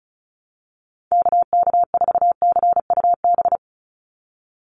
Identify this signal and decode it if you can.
Morse code, heard as a keyed tone, about 34 wpm, 700 Hz. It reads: KK4CUB
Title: USA Call Signs 35wpm